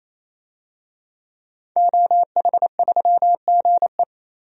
Morse code, heard as a keyed tone, about 28 wpm, 700 Hz. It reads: OH3GE
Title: DXCC-100 Call Signs - 5 Characters - 28wpm